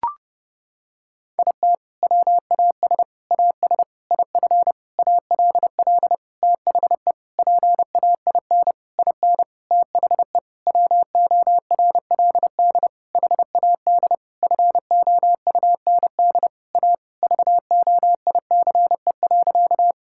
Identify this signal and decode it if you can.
Morse code, heard as a keyed tone, about 31 wpm, 700 Hz. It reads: IT WAS AS IF ALL THE PAIN IN THE WORLD HAD FOUND A VOICE.
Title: Sentences from Top 1600 Words 30wpm